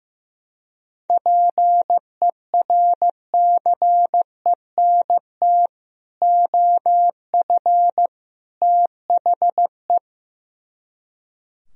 Morse code, heard as a keyed tone, about 15 wpm, 700 Hz. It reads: PERCENT OF THE
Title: Speed-Racing - Sets of 3 Words 15wpm